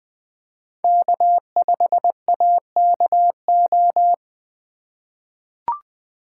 Morse code, heard as a keyed tone, about 20 wpm, 700 Hz. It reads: K5AKO
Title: USA Call Signs - 5 Characters - 2x Character Spacing - 20wpm